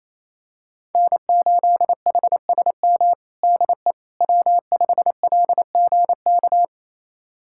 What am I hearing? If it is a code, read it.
Morse code, heard as a keyed tone, about 28 wpm, 700 Hz. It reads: N8HSM DE W5LGK
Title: USA Callsign DE Callsign - 5 Characters - 28wpm